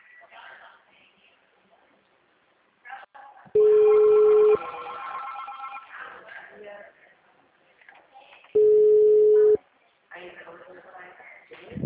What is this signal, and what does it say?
Ringback tone (Europe, 425 Hz), with speech.